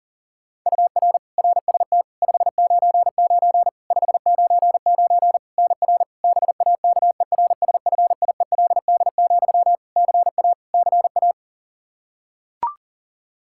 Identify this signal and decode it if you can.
Morse code, heard as a keyed tone, about 37 wpm, 700 Hz. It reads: UR RST 599 599 NR BAKERSFIELD, CA CA
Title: Practice QSO - Exchange 1 - Beginning - 40wpm